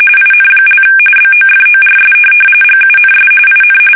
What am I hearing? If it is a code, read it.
RTTY, about 45.45 baud. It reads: IN) BOFORS GUNS IN
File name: fsk850 / USS-Massachusetts